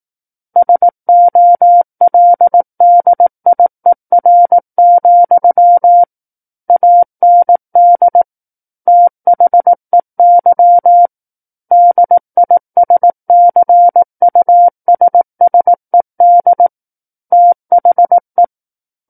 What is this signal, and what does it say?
Morse code, heard as a keyed tone, about 18 wpm, 700 Hz. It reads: SOLDIER, AND THEY DISCUSSED THE
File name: Book One / 09-Chapter 9 - 18 WPM